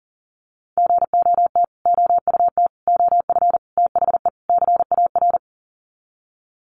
Morse code, heard as a keyed tone, about 40 wpm, 700 Hz. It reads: GOT OUT OF THE CAR
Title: Sets of 5 Words Encore 40wpm